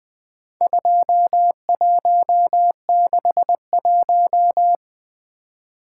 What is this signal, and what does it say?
Morse code, heard as a keyed tone, about 20 wpm, 700 Hz. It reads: 2161
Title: Four Letter-Number Combinations - 20wpm